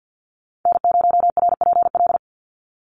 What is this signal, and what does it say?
Morse code, heard as a keyed tone, about 50 wpm, 700 Hz. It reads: N0RPL
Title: USA Call Signs 50wpm